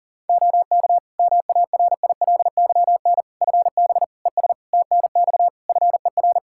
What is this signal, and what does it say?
Morse code, heard as a keyed tone, about 40 wpm, 700 Hz. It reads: OK MARILYN FB ES TNX FER
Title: Practice QSO - Exchange 2 - Info - 40wpm